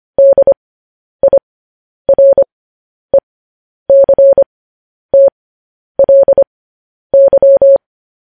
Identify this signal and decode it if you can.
Morse code, heard as a keyed tone, about 25 wpm, spaced as for 10 wpm, 550 Hz. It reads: DIRECTLY
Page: news-headlines-in-morse-code-at-10-wpm